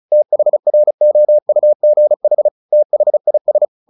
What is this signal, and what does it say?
Morse code, heard as a keyed tone, about 34 wpm, 600 Hz. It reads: THROUGH THIS
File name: the old man and the sea / the old man and the sea 1 35wpm